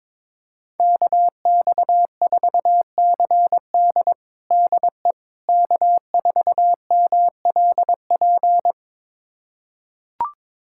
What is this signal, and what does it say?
Morse code, heard as a keyed tone, about 21 wpm, 700 Hz. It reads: KX4CD DE K4MLP
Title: USA Callsign DE Callsign - 5 Characters - 22wpm